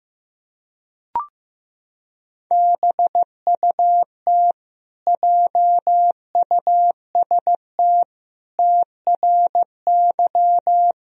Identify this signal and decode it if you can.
Morse code, heard as a keyed tone, about 15 wpm, 700 Hz. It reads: BUT JUST TRY
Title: Sentences from Top 1000 Words - Review Part 1 - 15wpm